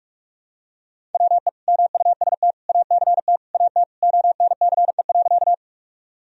Morse code, heard as a keyed tone, about 44 wpm, 700 Hz. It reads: WE MUST ACT AT ONCE.
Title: Sentences from Top 1000 Words - Review Part 1 - 45wpm